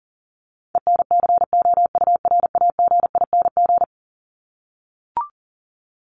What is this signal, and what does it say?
Morse code, heard as a keyed tone, about 40 wpm, 700 Hz. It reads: ENCOURAGING
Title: Top 1751-2000 Words 40wpm